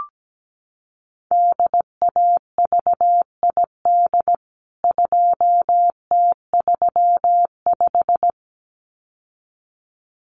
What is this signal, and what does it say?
Morse code, heard as a keyed tone, about 17 wpm, 700 Hz. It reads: DAVID 2T35
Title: CWT Exchange 17wpm